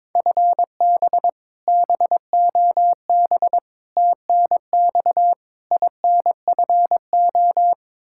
Morse code, heard as a keyed tone, about 21 wpm, 700 Hz. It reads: FB BOB TNX INFO
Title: Practice QSO - Exchange 3 - More Info - 22wpm